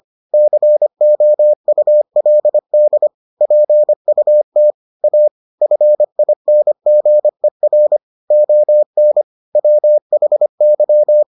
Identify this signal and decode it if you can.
Morse code, heard as a keyed tone, about 25 wpm, 600 Hz. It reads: COULD PUT A FINGER ON WHY